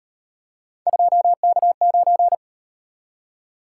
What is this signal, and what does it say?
Morse code, heard as a keyed tone, about 37 wpm, 700 Hz. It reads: 2K9